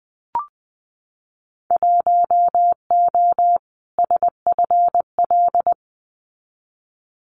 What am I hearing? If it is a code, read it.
Morse code, heard as a keyed tone, about 20 wpm, 700 Hz. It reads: 1O SFL
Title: Winter Field Day Exchanges 20wpm